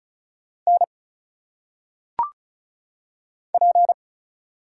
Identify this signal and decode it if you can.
Morse code, heard as a keyed tone, about 34 wpm, 700 Hz. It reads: N P
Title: Single Letters - Phonetic Pronunciation 35wpm